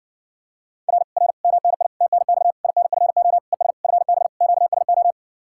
Morse code, heard as a keyed tone, about 60 wpm, 700 Hz. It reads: R R MNI TNX INFO ES FB QSO